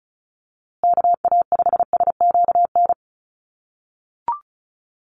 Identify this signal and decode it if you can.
Morse code, heard as a keyed tone, about 34 wpm, 700 Hz. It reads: KA5SQN
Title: USA Call Signs Repeated 3x 35wpm